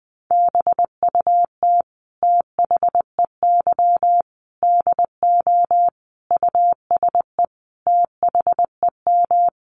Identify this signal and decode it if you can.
Morse code, heard as a keyed tone, about 19 wpm, 700 Hz. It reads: BUT THEY DO USE THEM
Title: Sentences from Top 1300 Words 20wpm